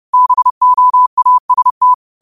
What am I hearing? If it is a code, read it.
Morse code, heard as a keyed tone, about 30 wpm, 1000 Hz. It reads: BOAST
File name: qotdfiles / 30wpm